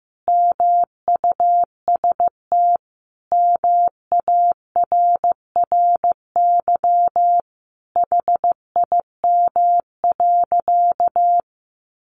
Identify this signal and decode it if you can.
Morse code, heard as a keyed tone, about 15 wpm, 700 Hz. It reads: MUST MARRY HIM.
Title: Sentences from Top 1200 Words 15wpm